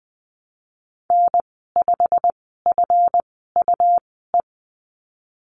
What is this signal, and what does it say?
Morse code, heard as a keyed tone, about 19 wpm, spaced as for 14 wpm, 700 Hz. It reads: N5FUE
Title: 147 Lesson 24g - Letter y - Call Signs 20wpm